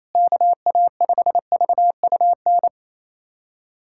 Morse code, heard as a keyed tone, about 28 wpm, 700 Hz. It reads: KA5VUN
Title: USA Callsign DE Callsign 28wpm